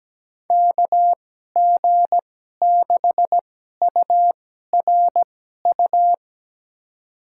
Morse code, heard as a keyed tone, about 17 wpm, spaced as for 12 wpm, 700 Hz. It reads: KG6URU
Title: USA Call Signs - 2x Character Spacing - 17wpm